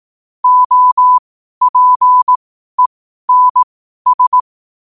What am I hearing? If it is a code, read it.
Morse code, heard as a keyed tone, about 18 wpm, spaced as for 12 wpm, 1000 Hz. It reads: OPENS
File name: qotdfiles / 12wpm